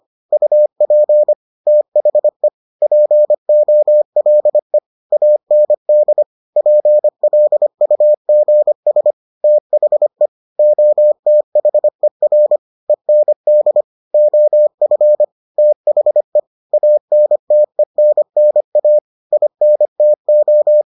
Morse code, heard as a keyed tone, about 25 wpm, 600 Hz. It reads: UP THE POLE AND PLUGS THE OTHER END OF THE ANTENNA INTO